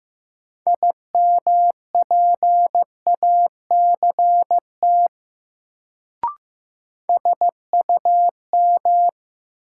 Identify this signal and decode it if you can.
Morse code, heard as a keyed tone, about 15 wpm, 700 Hz. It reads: IMPACT SUM
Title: Sentences from Top 1000 Words - Review Part 6 - New Word Drill - 15wpm